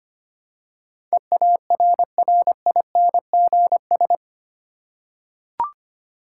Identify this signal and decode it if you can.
Morse code, heard as a keyed tone, about 25 wpm, 700 Hz. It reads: EARRINGS
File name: practice / 25wpm / Difficult Words with Hints - 25wpm